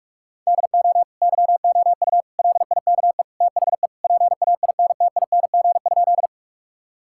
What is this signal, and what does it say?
Morse code, heard as a keyed tone, about 41 wpm, 700 Hz. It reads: DO YOU LIKE THE PAINTING?
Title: Sentences from Top 1000 Words - Review Part 6 - 45wpm